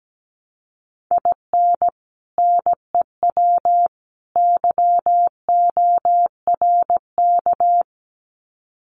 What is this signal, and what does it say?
Morse code, heard as a keyed tone, about 17 wpm, 700 Hz. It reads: IN NEW YORK
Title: Sets of 3 Words 17wpm